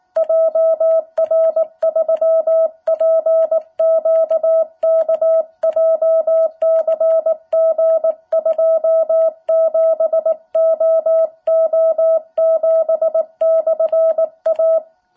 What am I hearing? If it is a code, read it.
Morse code, heard as a keyed tone, about 19 wpm, 660 Hz. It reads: JR3PQK JCG27OO7/A
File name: jh8cbh / 2021cw